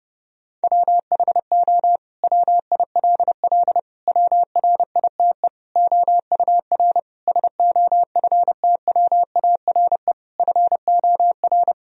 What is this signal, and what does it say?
Morse code, heard as a keyed tone, about 30 wpm, 700 Hz. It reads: WHO WILL WRITE OUR SOFTWARE FOR
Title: Sentences from Top 1000 Words - Review Part 3 - 30wpm